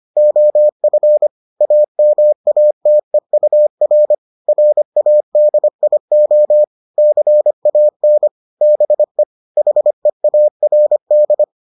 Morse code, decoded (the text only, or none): OF AMATEUR RADIO CAN BE HEARD